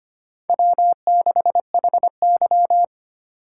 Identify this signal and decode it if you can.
Morse code, heard as a keyed tone, about 25 wpm, 700 Hz. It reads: W6HY
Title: USA Call Signs - 4 Characters - 4x Character Spacing - 25wpm